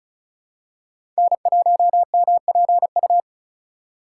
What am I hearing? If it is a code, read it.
Morse code, heard as a keyed tone, about 36 wpm, 700 Hz. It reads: N1MPU
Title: USA Call Signs - 2x Character Spacing - 35wpm